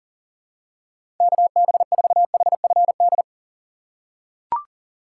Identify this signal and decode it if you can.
Morse code, heard as a keyed tone, about 40 wpm, 700 Hz. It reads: KB4HFD
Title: USA Call Signs - 6 Characters - 3x Character Spacing - 40wpm